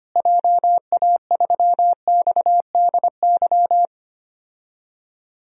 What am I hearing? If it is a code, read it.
Morse code, heard as a keyed tone, about 25 wpm, 700 Hz. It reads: JA3XDY
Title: DXCC-100 Call Signs - 6 Characters - Repeated 2x - 25wpm